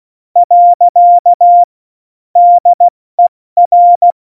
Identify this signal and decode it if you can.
Morse code, heard as a keyed tone, about 16 wpm, 700 Hz. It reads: . DER
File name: Morsekurs / Witze / Witz 22. Januar 2019, 70 F80